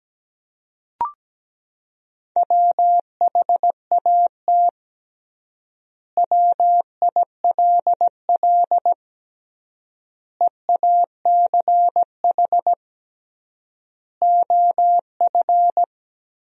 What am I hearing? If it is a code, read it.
Morse code, heard as a keyed tone, about 17 wpm, 700 Hz. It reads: WHAT WILL EACH OF
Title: 3x Word-Spacing - Sentences from Top 200 Words - 17wpm